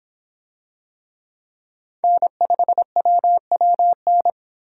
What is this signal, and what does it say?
Morse code, heard as a keyed tone, about 26 wpm, 700 Hz. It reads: N5WWN